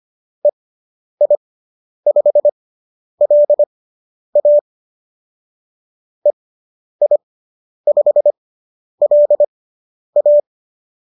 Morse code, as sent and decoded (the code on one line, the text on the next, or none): . .. ..... .-.. .- / . .. ..... .-.. .-
EI5LA EI5LA